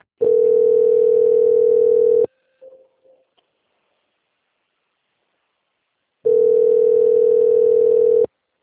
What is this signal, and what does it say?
Ringback tone (North America), 440 + 480 Hz.